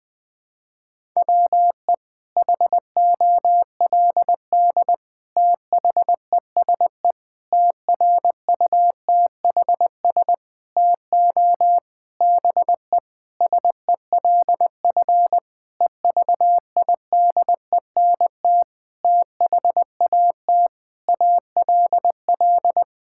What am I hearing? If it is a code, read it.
Morse code, heard as a keyed tone, about 20 wpm, 700 Hz. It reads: WE HOLD THESE TRUTHS TO BE SELF EVIDENT THAT ALL
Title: Sentences with 9-Word Phrases - 20wpm